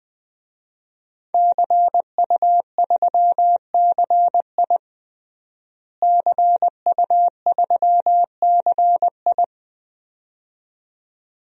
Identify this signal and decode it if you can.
Morse code, heard as a keyed tone, about 20 wpm, 700 Hz. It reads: CU3CI CU3CI
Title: DXCC-100 Call Signs - 5 Characters - Repeated 2x - 20wpm